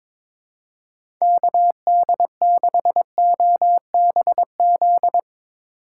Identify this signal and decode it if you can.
Morse code, heard as a keyed tone, about 21 wpm, 700 Hz. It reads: KD6OBZ
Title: USA Call Signs - 6 Characters - 3x Character Spacing - 22wpm